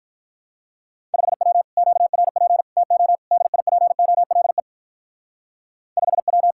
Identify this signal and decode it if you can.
Morse code, heard as a keyed tone, about 43 wpm, 700 Hz. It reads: 5W QRP TO DIPOLE 5W
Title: Speed-Racing Common QSO Phrases - 35wpm